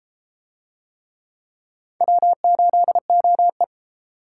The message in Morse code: W8OE

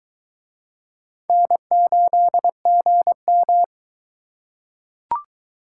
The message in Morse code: N8GM

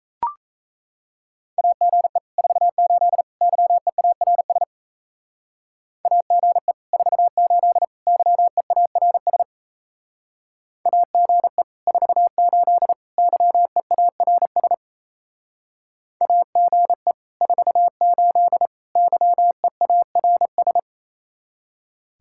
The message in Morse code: AGE 48 YEARS AGE 48 YEARS AGE 48 YEARS AGE 48 YEARS

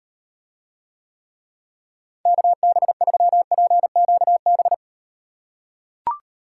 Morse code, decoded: KB3PQB

